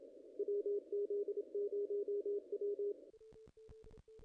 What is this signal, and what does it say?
Morse code, heard as a keyed tone, about 33 wpm, 410 Hz. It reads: WZ0W
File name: contests / ss-cw-k5tr-op-n5rz -0000